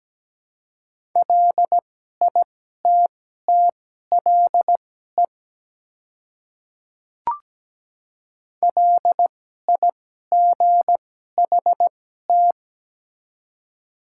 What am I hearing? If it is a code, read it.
Morse code, heard as a keyed tone, about 17 wpm, spaced as for 12 wpm, 700 Hz. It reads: LITTLE LIGHT
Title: 2x Character Spacing - Top 200 Words - 17wpm